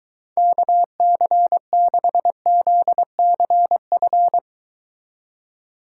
Morse code, heard as a keyed tone, about 23 wpm, 700 Hz. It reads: KC6ZCF